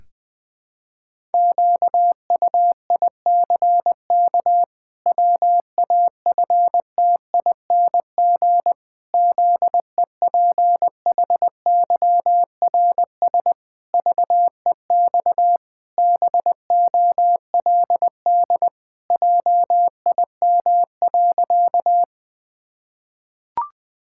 Morse code, heard as a keyed tone, about 20 wpm, 700 Hz. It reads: QUICK WAFTING ZEPHYRS VEX BOLD JIM.
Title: Pangram Challenge v2 - 20wpm